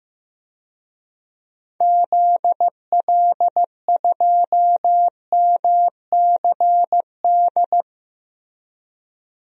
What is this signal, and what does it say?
Morse code, heard as a keyed tone, about 15 wpm, 700 Hz. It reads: ZL2MCD
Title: DXCC-100 Call Signs - 6 Characters - Repeated 2x - 15wpm